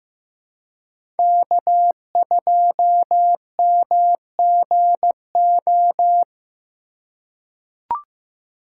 Morse code, heard as a keyed tone, about 15 wpm, 700 Hz. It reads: K2MGO
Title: USA Call Signs - 5 Characters - 4x Character Spacing - 15wpm